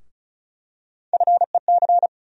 Morse code, heard as a keyed tone, about 36 wpm, 700 Hz. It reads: FEC
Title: USA Call Sign Postfixes - 3 Letters 35wpm